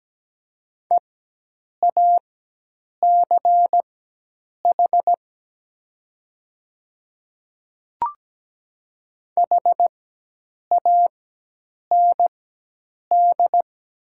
Morse code, heard as a keyed tone, about 17 wpm, spaced as for 8 wpm, 700 Hz. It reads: EACH HAND
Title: 4x Character Spacing - Top 200 Words - 17wpm